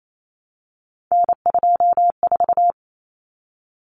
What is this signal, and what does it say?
Morse code, heard as a keyed tone, about 28 wpm, 700 Hz. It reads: N24
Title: Common Cut Numbers - 3-Digit 28wpm